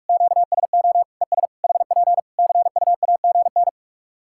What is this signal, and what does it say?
Morse code, heard as a keyed tone, about 46 wpm, 700 Hz. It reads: QSO ES HP CUAGN
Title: K1USN SST QSO Practice 45wpm